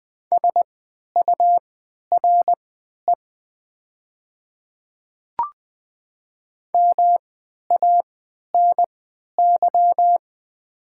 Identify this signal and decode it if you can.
Morse code, heard as a keyed tone, about 20 wpm, spaced as for 11 wpm, 700 Hz. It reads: SURE MANY
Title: 3x Character Spacing - Top 500 Words - 20wpm